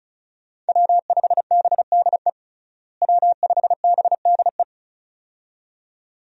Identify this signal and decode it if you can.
Morse code, heard as a keyed tone, about 34 wpm, 700 Hz. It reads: W5BDE W5BDE
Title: USA Call Signs - 5 Characters - Repeated 2x - 35wpm